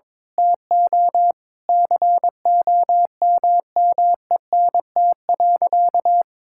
Morse code, heard as a keyed tone, about 22 wpm, 700 Hz. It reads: TO COMMENT.
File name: practice / 22wpm / Sentences with 8-Word Phrases - 22wpm